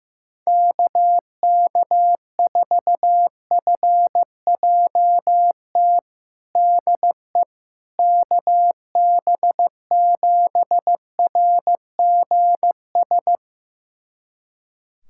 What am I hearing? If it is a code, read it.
Morse code, heard as a keyed tone, about 16 wpm, 700 Hz. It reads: KK4FJT DE KB7RGS
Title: USA Callsign DE Callsign - 6 Characters - 15wpm